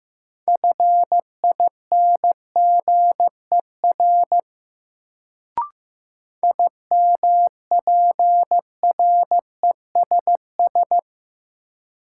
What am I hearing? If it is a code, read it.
Morse code, heard as a keyed tone, about 15 wpm, 700 Hz. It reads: FINGER IMPRESS
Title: Sentences from Top 1600 Words - New Word Drill - 15wpm